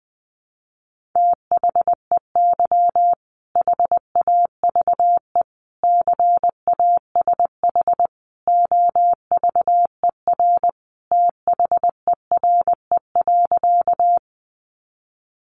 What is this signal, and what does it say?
Morse code, heard as a keyed tone, about 20 wpm, 700 Hz. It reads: THEY HAVE CASH OVER THERE.
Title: Sentences from Top 700 Words 20wpm